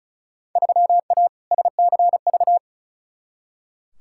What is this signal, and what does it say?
Morse code, heard as a keyed tone, about 36 wpm, 700 Hz. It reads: 3A SCV
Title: Field Day Exchanges - 35wpm